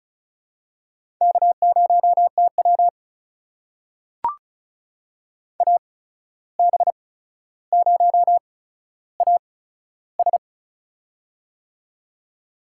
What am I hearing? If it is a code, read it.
Morse code, heard as a keyed tone, about 36 wpm, spaced as for 14 wpm, 700 Hz. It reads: K0TW AB0AS
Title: USA Call Signs - 8x Character Spacing - 35wpm